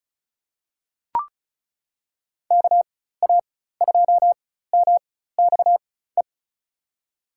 Morse code, heard as a keyed tone, about 34 wpm, spaced as for 16 wpm, 700 Hz. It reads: KA2MXE
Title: USA Call Signs - 6 Characters - 4x Character Spacing - 35wpm